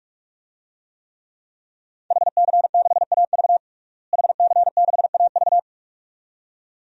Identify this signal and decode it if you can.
Morse code, heard as a keyed tone, about 46 wpm, 700 Hz. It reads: HC6AV HC6AV
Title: DXCC-100 Call Signs - 5 Characters - Repeated 2x - 45wpm